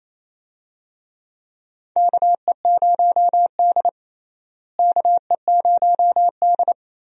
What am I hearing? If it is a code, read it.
Morse code, heard as a keyed tone, about 28 wpm, 700 Hz. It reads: KE0D KE0D
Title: USA Call Signs - 4 Characters - Repeated 3x - 28wpm